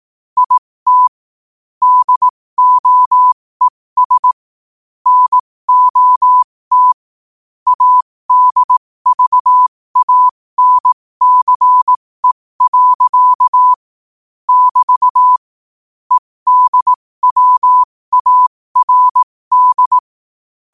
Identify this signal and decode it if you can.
Morse code, heard as a keyed tone, about 18 wpm, spaced as for 15 wpm, 1000 Hz. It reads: IT DOES NOT ADVANCE. <BT> EDWARD
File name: qotdfiles / 15wpm